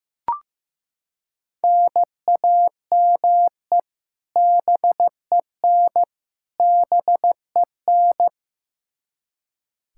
Morse code, heard as a keyed tone, about 15 wpm, 700 Hz. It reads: NAME BEN BEN
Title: K1USN SST QSO Practice 15wpm